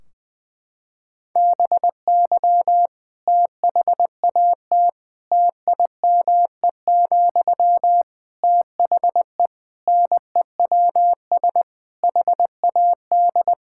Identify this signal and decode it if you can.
Morse code, heard as a keyed tone, about 20 wpm, 700 Hz. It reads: BY THAT TIME, THE NEWS HAD
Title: Sentences from Top 1600 Words 20wpm